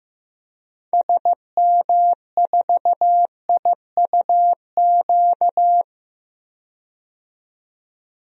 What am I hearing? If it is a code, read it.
Morse code, heard as a keyed tone, about 15 wpm, 700 Hz. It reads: SM4IUQ
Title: DXCC-100 Call Signs - 6 Characters - 15wpm